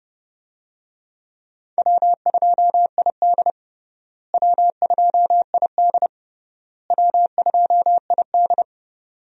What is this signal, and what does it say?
Morse code, heard as a keyed tone, about 30 wpm, 700 Hz. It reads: W2ID W2ID W2ID
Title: USA Call Signs - 4 Characters - Repeated 3x - 30wpm